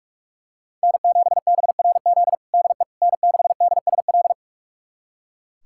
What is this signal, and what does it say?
Morse code, heard as a keyed tone, about 41 wpm, 700 Hz. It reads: N7BRZ DE N6DSL
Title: USA Callsign DE Callsign - 5 Characters - 45wpm